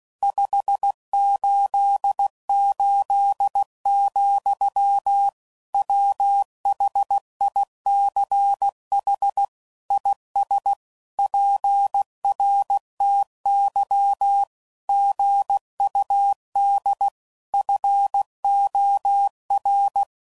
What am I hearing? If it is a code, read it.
Morse code, heard as a keyed tone, about 16 wpm, 800 Hz. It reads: 588, WHICH IS PRTY GUD FOR
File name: CW Code Groups @15WPM / QSOs / WA7QOX